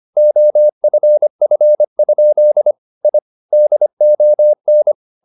Morse code, heard as a keyed tone, about 25 wpm, 600 Hz. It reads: OFF? I DON
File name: the old man and the sea part 1 25wpm